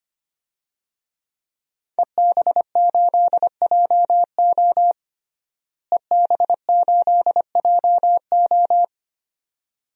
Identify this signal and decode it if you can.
Morse code, heard as a keyed tone, about 25 wpm, 700 Hz. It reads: EB8JO EB8JO
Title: DXCC-100 Call Signs - 5 Characters - Repeated 2x - 25wpm